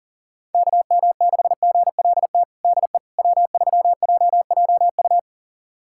Morse code, heard as a keyed tone, about 40 wpm, 700 Hz. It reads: KM6GLT DE W3JJU
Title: USA Callsign DE Callsign 40wpm